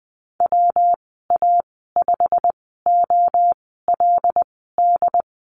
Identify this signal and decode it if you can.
Morse code, heard as a keyed tone, about 20 wpm, spaced as for 15 wpm, 700 Hz. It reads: WA5OLD
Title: 147 Lesson 24g - Letter y - Call Signs 20wpm